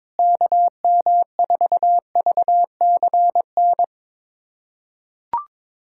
Morse code, heard as a keyed tone, about 22 wpm, 700 Hz. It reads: KM4VCN
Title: USA Callsign DE Callsign - 6 Characters - 22wpm